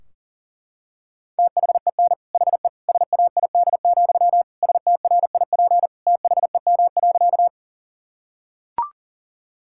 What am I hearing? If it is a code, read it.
Morse code, heard as a keyed tone, about 37 wpm, 700 Hz. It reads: THEN HE SAID, STRIP THEM.
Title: Sentences from Top 1600 Words 40wpm